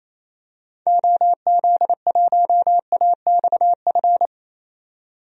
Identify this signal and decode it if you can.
Morse code, heard as a keyed tone, about 27 wpm, 700 Hz. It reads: OZ1AXF
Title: DXCC-100 Call Signs - 6 Characters - 28wpm